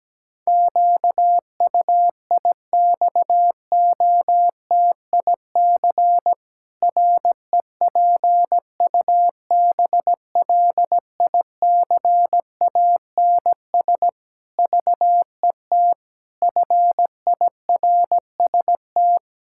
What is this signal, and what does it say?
Morse code, heard as a keyed tone, about 17 wpm, 700 Hz. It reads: QUIXOTIC REPUBLICANS VET FIRST
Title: Pangram Challenge v2 - 17wpm